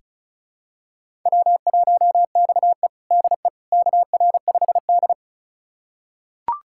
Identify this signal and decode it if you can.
Morse code, heard as a keyed tone, about 34 wpm, 700 Hz. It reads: W1XE DE KR5D
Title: USA Callsign DE Callsign - 4 Characters - 35wpm